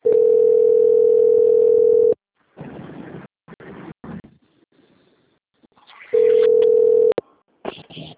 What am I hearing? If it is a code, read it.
Ringback tone (North America), 440 + 480 Hz.